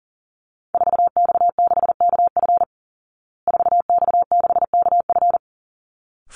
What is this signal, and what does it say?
Morse code, heard as a keyed tone, about 40 wpm, 700 Hz. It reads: 4X6KF 4X6KF
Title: Call Signs Repeated 3x - 40wpm